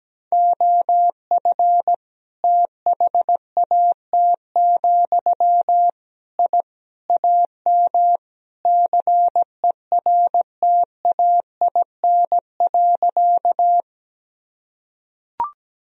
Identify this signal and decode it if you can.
Morse code, heard as a keyed tone, about 17 wpm, 700 Hz. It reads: OF THAT, I AM CERTAIN.
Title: Sentences from Top 1000 Words - Review Part 2 - 17wpm